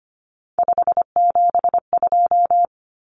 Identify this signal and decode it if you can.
Morse code, heard as a keyed tone, about 25 wpm, 700 Hz. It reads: 572
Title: Common Cut Numbers - 3-Digit 25wpm